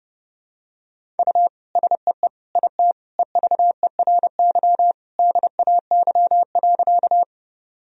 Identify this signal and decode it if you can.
Morse code, heard as a keyed tone, about 30 wpm, 700 Hz. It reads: U SEE IT EVERY DAY.
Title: Sentences from Top 200 Words and Abbreviations 30wpm